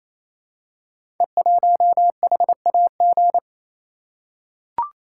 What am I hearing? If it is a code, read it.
Morse code, heard as a keyed tone, about 27 wpm, 700 Hz. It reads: E1HAG
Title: Code Groups - 5 Letter:Numbers - 28wpm